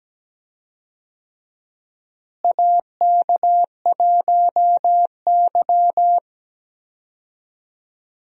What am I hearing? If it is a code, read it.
Morse code, heard as a keyed tone, about 17 wpm, 700 Hz. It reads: AK1Y
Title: USA Call Signs - 4 Characters - Repeated 3x - 17wpm